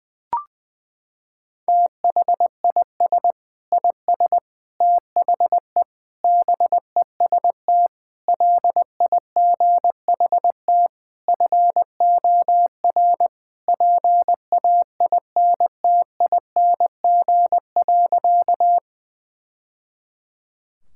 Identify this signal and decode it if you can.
Morse code, heard as a keyed tone, about 20 wpm, 700 Hz. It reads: THIS IS THE BEST LIGHT FOR PAINTING.
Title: Sentences from Top 1000 Words - Review Part 6 - 20wpm